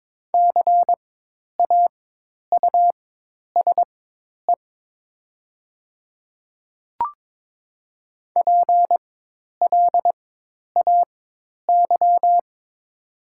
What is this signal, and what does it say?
Morse code, heard as a keyed tone, about 22 wpm, spaced as for 10 wpm, 700 Hz. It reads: CAUSE PLAY
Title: 4x Character Spacing - Top 500 Words - 22wpm